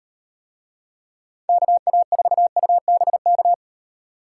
Morse code, heard as a keyed tone, about 35 wpm, 700 Hz. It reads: KA4UBK